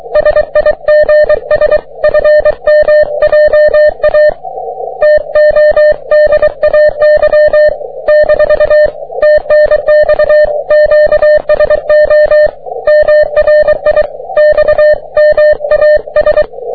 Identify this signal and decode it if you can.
Morse code, heard as a keyed tone, about 23 wpm, 600 Hz. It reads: SIGS FM JA TODAY - TNX QSO MRI XMAS
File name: wp / jg3qzn 30m cw